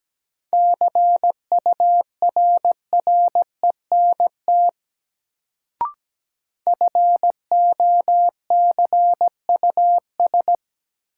Morse code, heard as a keyed tone, about 17 wpm, 700 Hz. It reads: CURRENT FOCUS
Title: Sentences from Top 1000 Words - Review Part 4 - New Word Drill - 17wpm